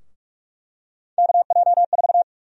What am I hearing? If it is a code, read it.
Morse code, heard as a keyed tone, about 42 wpm, 700 Hz. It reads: KJ4
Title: USA Call Sign Prefixes - 45wpm